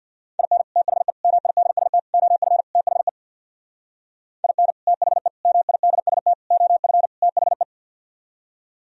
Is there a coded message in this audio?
Yes: Morse code from 0.4 to 7.6 s, heard as a keyed tone, about 64 wpm, 700 Hz. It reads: IN THE MIDST OF THE IN THE MIDST OF THE